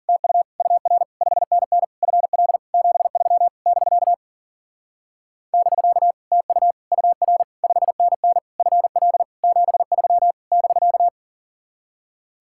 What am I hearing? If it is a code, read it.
Morse code, heard as a keyed tone, about 50 wpm, 700 Hz. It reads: TU UR 5NN FL 73 <BK> <BK> TU UR 5NN FL 73 <BK>
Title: Speed-Racing Common QSO Phrases - 40wpm